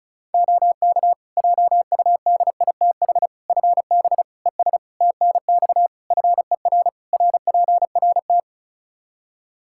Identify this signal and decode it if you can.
Morse code, heard as a keyed tone, about 36 wpm, 700 Hz. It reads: OK JUDITH FB ES TNX FER RPRT
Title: Practice QSO - Exchange 2 - Info - 35wpm